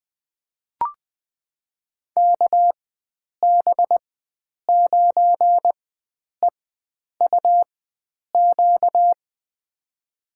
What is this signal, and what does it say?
Morse code, heard as a keyed tone, about 20 wpm, spaced as for 9 wpm, 700 Hz. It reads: KB9EUQ
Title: USA Call Signs - 6 Characters - 4x Character Spacing - 20wpm